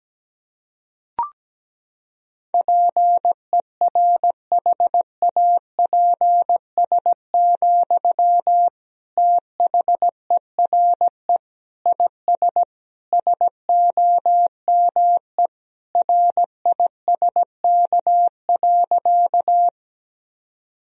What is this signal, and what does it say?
Morse code, heard as a keyed tone, about 17 wpm, 700 Hz. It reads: PERHAPS, THERE IS SOME RISK.
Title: Sentences from Top 1000 Words - Review Part 4 - 17wpm